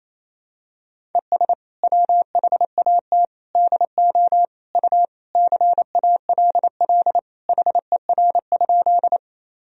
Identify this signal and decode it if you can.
Morse code, heard as a keyed tone, about 28 wpm, 700 Hz. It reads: ES WHAT DO U CALL HER?
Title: Sentences from Top 100 Words and Abbreviations 28wpm